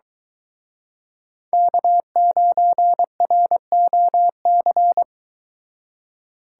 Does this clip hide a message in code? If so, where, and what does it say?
Yes: Morse code from 1.5 to 5.0 s, heard as a keyed tone, about 22 wpm, 700 Hz. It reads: K9ROC